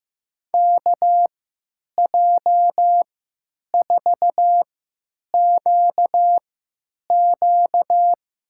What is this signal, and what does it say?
Morse code, heard as a keyed tone, about 15 wpm, spaced as for 9 wpm, 700 Hz. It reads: KJ4QQ
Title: USA Call Signs - 3x Character Spacing - 15wpm